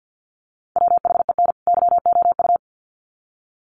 Morse code, heard as a keyed tone, about 50 wpm, 700 Hz. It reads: WHEN YOU
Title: Sets of 2 Words 50wpm